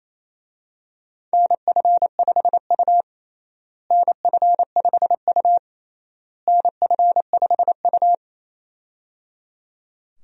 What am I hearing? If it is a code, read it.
Morse code, heard as a keyed tone, about 28 wpm, 700 Hz. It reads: NF5U NF5U NF5U
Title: USA Call Signs - 4 Characters - Repeated 3x - 28wpm